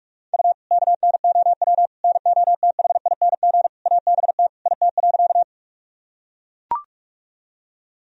Morse code, heard as a keyed tone, about 44 wpm, 700 Hz. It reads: U KNOW NOTHING ABT IT.
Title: Sentences from Top 500 Words and Abbreviations 45wpm